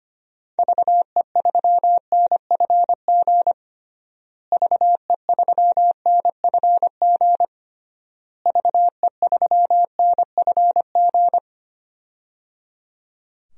Morse code, heard as a keyed tone, about 25 wpm, 700 Hz. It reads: VE3NFG VE3NFG VE3NFG
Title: DXCC-100 Call Signs - 6 Characters - Repeated 3x - 25wpm